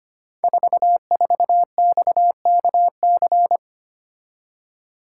Morse code, heard as a keyed tone, about 25 wpm, 700 Hz. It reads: 44XKC
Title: Code Groups - 5 Letter:Numbers - 25wpm